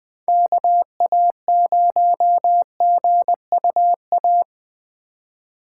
Morse code, heard as a keyed tone, about 21 wpm, 700 Hz. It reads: KA0GUA
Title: USA Call Signs - 6 Characters - 4x Character Spacing - 20wpm